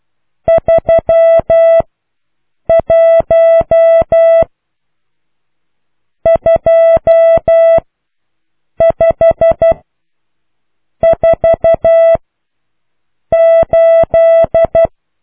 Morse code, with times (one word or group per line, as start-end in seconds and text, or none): 0.5-1.8 3
2.7-4.4 1
6.3-7.8 2
8.8-9.7 5
11.0-12.2 4
13.3-14.9 8